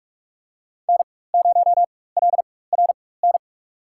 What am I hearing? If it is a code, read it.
Morse code, heard as a keyed tone, about 44 wpm, spaced as for 21 wpm, 700 Hz. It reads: N0LRN
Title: USA Call Signs - 5 Characters - 4x Character Spacing - 45wpm